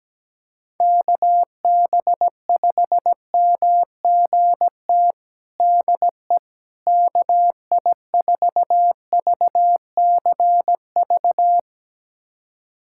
Morse code, heard as a keyed tone, about 17 wpm, 700 Hz. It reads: KB5MGT DE KI4VCV
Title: USA Callsign DE Callsign - 6 Characters - 17wpm